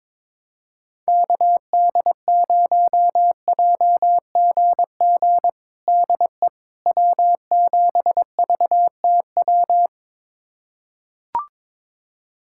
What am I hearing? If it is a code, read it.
Morse code, heard as a keyed tone, about 22 wpm, 700 Hz. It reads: KD0JGG DE W7VTW
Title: USA Callsign DE Callsign 22wpm